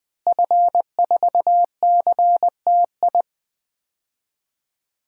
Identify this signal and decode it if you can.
Morse code, heard as a keyed tone, about 21 wpm, 700 Hz. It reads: F4CTI
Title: DXCC-100 Call Signs - 5 Characters - Repeated 3x - 20wpm